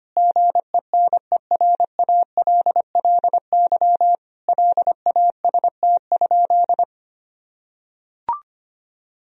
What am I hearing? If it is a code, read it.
Morse code, heard as a keyed tone, about 25 wpm, 700 Hz. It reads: GENERALLY LAST?
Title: Sentences from Top 1000 Words - Review Part 4 - 25wpm